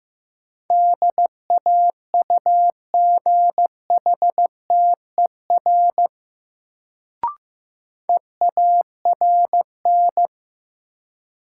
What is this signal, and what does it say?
Morse code, heard as a keyed tone, about 16 wpm, 700 Hz. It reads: DAUGHTER EARN
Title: Sentences from Top 1500 Words - New Word Drill - 15wpm